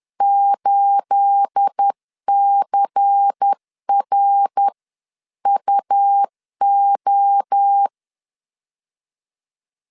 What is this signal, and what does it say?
Morse code, heard as a keyed tone, about 10 wpm, spaced as for 9 wpm, 800 Hz. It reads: 8CRUO
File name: LEZIONI / 9 28 10wpm 2842